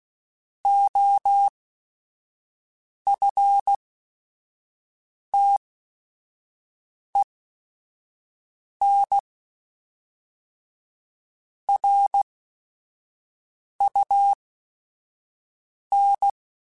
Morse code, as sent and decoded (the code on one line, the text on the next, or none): --- ..-. - . -. / .-. ..- -.
OFTEN RUN